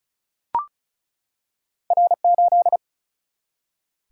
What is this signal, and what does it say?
Morse code, heard as a keyed tone, about 36 wpm, 700 Hz. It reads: R8
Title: Single-Letter Single-Number 35wpm